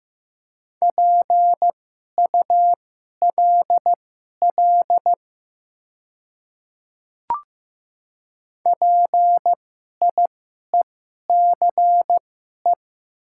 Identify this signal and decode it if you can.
Morse code, heard as a keyed tone, about 15 wpm, spaced as for 11 wpm, 700 Hz. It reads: PULL PIECE
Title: 2x Character Spacing - Top 500 Words - 15wpm